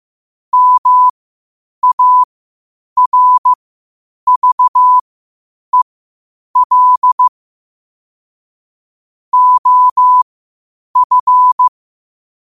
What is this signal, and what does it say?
Morse code, heard as a keyed tone, about 15 wpm, spaced as for 8 wpm, 1000 Hz. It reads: MARVEL OF
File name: qotdfiles / 8wpm